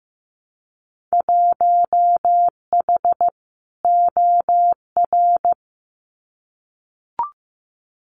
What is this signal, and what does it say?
Morse code, heard as a keyed tone, about 15 wpm, 700 Hz. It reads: 1H OR
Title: Winter Field Day Exchanges 15wpm